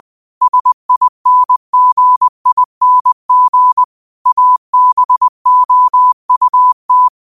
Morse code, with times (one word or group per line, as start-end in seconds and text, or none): 0.4-3.8 SINGING
4.3-7.1 ABOUT